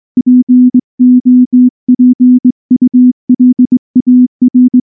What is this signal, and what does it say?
Morse code, heard as a keyed tone, about 20 wpm, 250 Hz. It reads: POPULAR
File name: morse-training-20WPM